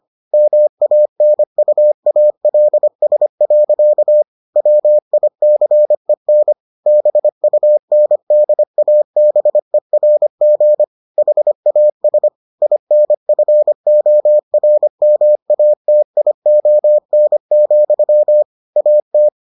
Morse code, heard as a keyed tone, about 25 wpm, 600 Hz. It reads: MANUALS. WICEN BUNDABERG HAS INFORMATION, AT